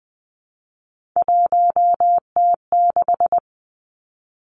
Morse code, heard as a keyed tone, about 20 wpm, 700 Hz. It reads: 1T6
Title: Three Challenging Letter-Number Combinations 20wpm